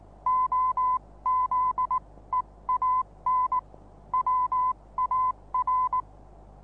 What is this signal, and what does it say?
Morse code, heard as a keyed tone, about 19 wpm, spaced as for 16 wpm, 1010 Hz. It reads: OZEAN WAR
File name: repos / Morsekurs / DJ4UF Morsen26